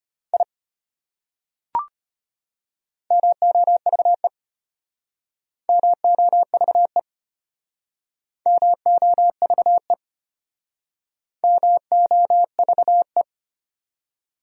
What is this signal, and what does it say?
Morse code, heard as a keyed tone, about 32 wpm, 700 Hz. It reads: I MOVE MOVE MOVE MOVE